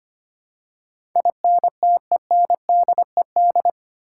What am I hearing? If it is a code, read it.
Morse code, heard as a keyed tone, about 25 wpm, 700 Hz. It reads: INTENDED
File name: Eight Letter Words - 25wpm